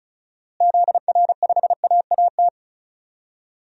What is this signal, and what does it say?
Morse code, heard as a keyed tone, about 35 wpm, 700 Hz. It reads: ZR5AAT
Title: DXCC-100 Call Signs - 6 Characters - Repeated 2x - 35wpm